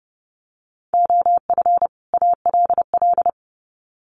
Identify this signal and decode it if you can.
Morse code, heard as a keyed tone, about 30 wpm, 700 Hz. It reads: OF ALL
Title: Sets of 2 Words 30wpm